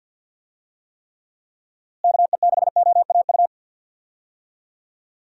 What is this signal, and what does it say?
Morse code, heard as a keyed tone, about 47 wpm, 700 Hz. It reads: KE6OAU
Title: USA Call Signs - 6 Characters - Repeated 3x - 50wpm